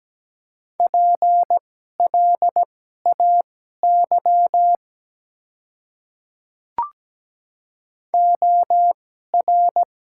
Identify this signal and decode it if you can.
Morse code, heard as a keyed tone, about 17 wpm, spaced as for 12 wpm, 700 Hz. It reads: PLAY OR
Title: 2x Character Spacing - Top 300 Words - 17wpm